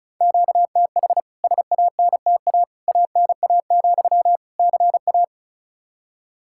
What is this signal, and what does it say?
Morse code, heard as a keyed tone, about 36 wpm, 700 Hz. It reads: QTH SANTA ANA, CA
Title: K1USN SST QSO Practice 35wpm